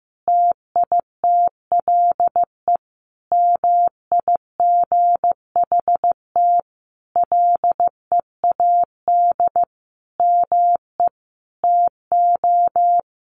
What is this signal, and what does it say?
Morse code, heard as a keyed tone, about 15 wpm, 700 Hz. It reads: TITLE MIGHT LEAD ME TO
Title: Sentences from Top 1200 Words 15wpm